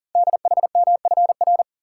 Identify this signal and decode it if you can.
Morse code, heard as a keyed tone, about 41 wpm, 700 Hz. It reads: DHMFR
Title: Code Groups - 5 Letters - 40wpm